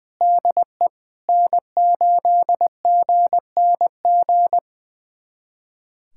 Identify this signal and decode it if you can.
Morse code, heard as a keyed tone, about 20 wpm, 700 Hz. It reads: DE N8GNG
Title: USA Callsign DE Callsign 20wpm